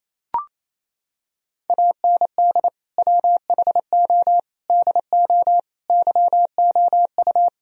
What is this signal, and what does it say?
Morse code, heard as a keyed tone, about 28 wpm, 700 Hz. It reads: AND WHO DO YOU
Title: Sentences from Top 1000 Words - Review Part 3 - 28wpm